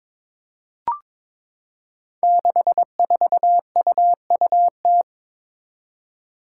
Morse code, heard as a keyed tone, about 22 wpm, 700 Hz. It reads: 64UUT
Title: Five Challenging Letter-Number Combinations 22wpm